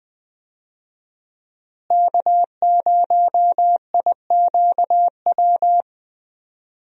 Morse code, heard as a keyed tone, about 20 wpm, 700 Hz. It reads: K0IQW
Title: USA Call Signs - 5 Characters - Repeated 2x - 20wpm